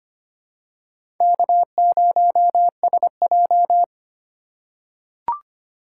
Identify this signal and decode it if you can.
Morse code, heard as a keyed tone, about 25 wpm, 700 Hz. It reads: K0SJ
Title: USA Call Signs - 4 Characters - 2x Character Spacing - 25wpm